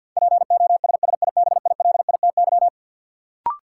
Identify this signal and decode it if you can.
Morse code, heard as a keyed tone, about 53 wpm, 700 Hz. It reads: POSSIBILITY
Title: Sets of 3 Words - New Word Drill - 50wpm